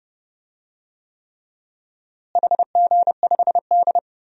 Morse code, heard as a keyed tone, about 32 wpm, 700 Hz. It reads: HG5D
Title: DXCC-100 Call Signs - 4 Characters - Repeated 3x - 30wpm